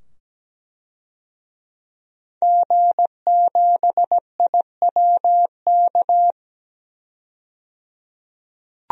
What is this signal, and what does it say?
Morse code, heard as a keyed tone, about 17 wpm, 700 Hz. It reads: G7IWK
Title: DXCC-100 Call Signs - 5 Characters - 17wpm